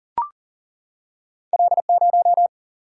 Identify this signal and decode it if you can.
Morse code, heard as a keyed tone, about 39 wpm, 700 Hz. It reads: L0
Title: Single-Letter Single-Number 40wpm